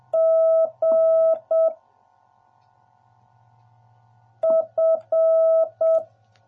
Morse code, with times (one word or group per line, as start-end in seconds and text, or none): 0.2-1.7 G
4.4-6.0 F